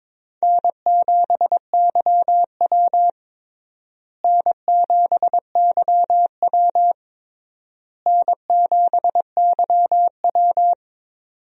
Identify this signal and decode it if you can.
Morse code, heard as a keyed tone, about 23 wpm, 700 Hz. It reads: N7YW N7YW N7YW
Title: USA Call Signs - 4 Characters - Repeated 3x - 22wpm